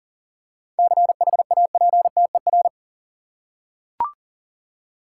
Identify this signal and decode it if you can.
Morse code, heard as a keyed tone, about 40 wpm, 700 Hz. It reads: CHAPTER
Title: Seven Letter Words - 40wpm